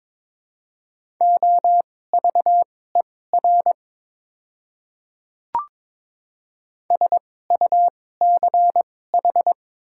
Morse code, heard as a keyed tone, about 22 wpm, spaced as for 16 wpm, 700 Hz. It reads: OVER SUCH
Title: 2x Character Spacing - Top 300 Words - 22wpm